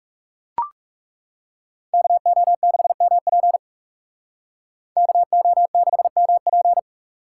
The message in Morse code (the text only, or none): KO6MP KO6MP